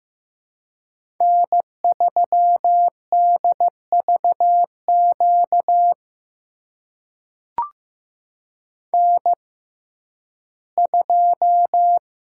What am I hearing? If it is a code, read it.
Morse code, heard as a keyed tone, about 15 wpm, spaced as for 11 wpm, 700 Hz. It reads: N3DVQ N2
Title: USA Call Signs - 5 Characters - 6x Character Spacing - 15wpm